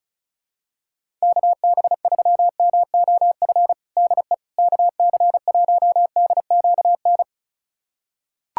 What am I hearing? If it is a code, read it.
Morse code, heard as a keyed tone, about 36 wpm, 700 Hz. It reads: KB3MOF DE KC1DQN
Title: USA Callsign DE Callsign 35wpm